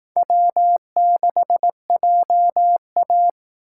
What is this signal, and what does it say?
Morse code, heard as a keyed tone, about 18 wpm, 700 Hz. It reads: W6JA
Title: Speed Racing - USA 1x2 Call Signs 15wpm